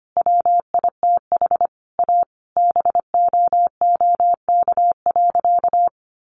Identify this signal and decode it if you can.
Morse code, heard as a keyed tone, about 25 wpm, 700 Hz. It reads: WITH A BOOK.
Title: Easy Phrases 25wpm